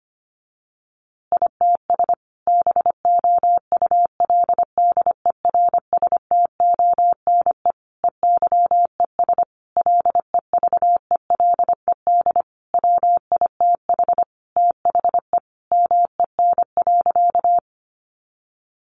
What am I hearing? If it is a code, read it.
Morse code, heard as a keyed tone, about 25 wpm, 700 Hz. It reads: ITS BOULDERSTONE EYES LEVELED WITH THE MEN.
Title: A Sound Of Thunder 25wpm